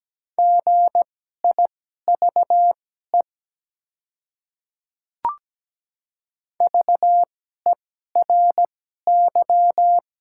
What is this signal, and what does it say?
Morse code, heard as a keyed tone, about 17 wpm, spaced as for 12 wpm, 700 Hz. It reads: GIVE VERY
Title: 2x Character Spacing - Top 200 Words - 17wpm